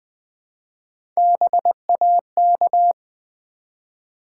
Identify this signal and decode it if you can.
Morse code, heard as a keyed tone, about 20 wpm, 700 Hz. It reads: BAK
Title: Build A Word - 1 Letter At A Time - 4-Letter Words - 20wpm